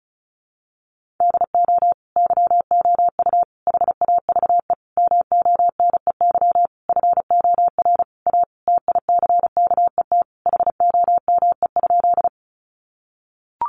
Morse code, heard as a keyed tone, about 34 wpm, 700 Hz. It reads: DO YOU HAVE MONEY FOR A TICKET HOME?
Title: Sentences from Top 1400 Words 35wpm